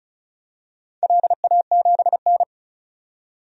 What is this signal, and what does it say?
Morse code, heard as a keyed tone, about 36 wpm, 700 Hz. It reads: LA7N
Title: DXCC-100 Call Signs - 4 Characters - 35wpm